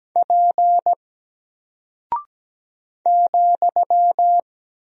Morse code, heard as a keyed tone, about 17 wpm, 700 Hz. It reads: P ,
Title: Single Character (Letter:Number:Punctuation:Prosign) Alt 17wpm